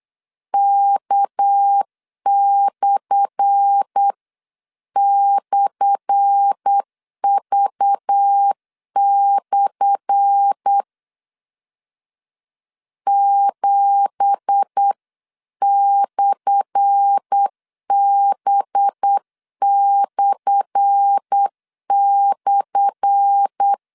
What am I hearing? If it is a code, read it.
Morse code, heard as a keyed tone, about 8 wpm, 800 Hz. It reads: K//V/ 7/B//
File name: LEZIONI / 10 SIGNS / 10A SLASH 8wpm 3452